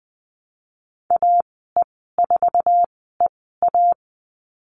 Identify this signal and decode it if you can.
Morse code, heard as a keyed tone, about 20 wpm, spaced as for 15 wpm, 700 Hz. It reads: AE4EA
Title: 055 Lesson 10g - Letter r - Call Signs 20wpm